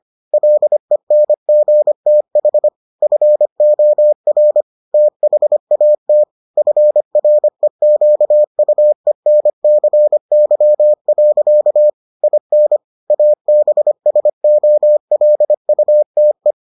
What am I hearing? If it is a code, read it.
Morse code, heard as a keyed tone, about 25 wpm, 600 Hz. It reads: LENGTH FOR THAT FREQUENCY. IN ABSOLUTE